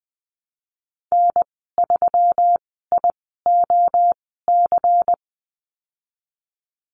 Morse code, heard as a keyed tone, about 20 wpm, spaced as for 15 wpm, 700 Hz. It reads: N3IOC
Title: 186 Lesson 30g - Letter b - Call Signs 20wpm